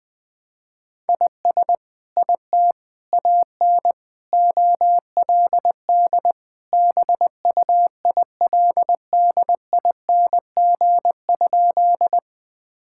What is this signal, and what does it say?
Morse code, heard as a keyed tone, about 20 wpm, 700 Hz. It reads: IS IT AN OLD BUILDING?
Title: Sentences from Top 1000 Words - Review Part 5 - 20wpm